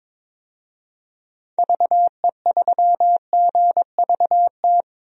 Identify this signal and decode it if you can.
Morse code, heard as a keyed tone, about 22 wpm, 700 Hz. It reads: VE3GVT
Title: DXCC-100 Call Signs - 6 Characters - Repeated 3x - 22wpm